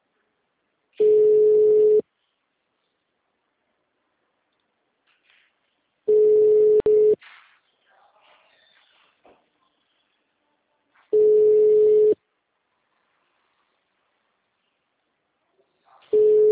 Ringback tone (Europe, 425 Hz).